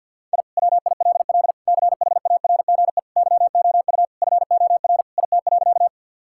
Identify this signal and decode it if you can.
Morse code, heard as a keyed tone, about 52 wpm, 700 Hz. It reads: I WILL CHARGE YOU FOR IT.
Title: Sentences from Top 1000 Words - Review Part 6 - 50wpm